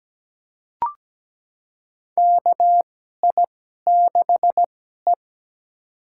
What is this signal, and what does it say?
Morse code, heard as a keyed tone, about 17 wpm, spaced as for 12 wpm, 700 Hz. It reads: KI6E
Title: USA Call Signs - 4 Characters - 2x Character Spacing - 17wpm